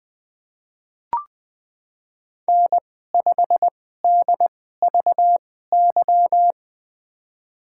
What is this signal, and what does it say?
Morse code, heard as a keyed tone, about 20 wpm, spaced as for 14 wpm, 700 Hz. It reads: N5DVY
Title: USA Call Signs - 5 Characters - 2x Character Spacing - 20wpm